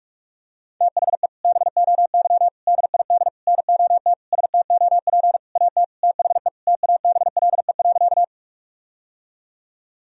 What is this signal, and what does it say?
Morse code, heard as a keyed tone, about 46 wpm, 700 Hz. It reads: THE BOY DID NOT STOP AT THE TABLE.
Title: Sentences from Top 1000 Words - Review Part 2 - 45wpm